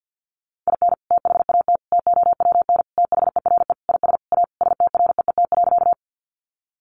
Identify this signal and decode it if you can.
Morse code, heard as a keyed tone, about 50 wpm, 700 Hz. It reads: IN THAT TOWN THERE IS A STREET.
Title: Sentences from Top 500 Words 50wpm